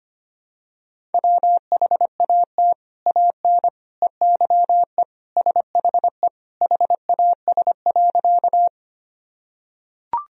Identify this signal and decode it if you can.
Morse code, heard as a keyed tone, about 25 wpm, 700 Hz. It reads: WHAT AN EYE SHE HAS.
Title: Sentences from Top 1000 Words - Review Part 1 - 25wpm